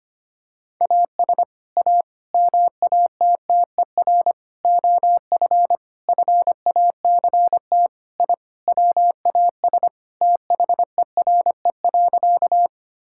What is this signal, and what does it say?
Morse code, heard as a keyed tone, about 25 wpm, 700 Hz. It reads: AS A MATTER OF FACT I WAS THERE.
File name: practice / 25wpm / Sentences with 4-Word Phrases - 25wpm